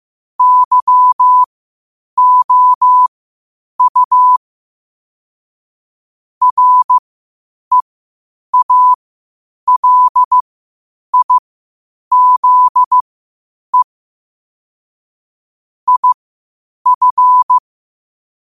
Morse code, heard as a keyed tone, about 15 wpm, spaced as for 8 wpm, 1000 Hz. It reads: YOU REALIZE IF
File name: qotdfiles / 8wpm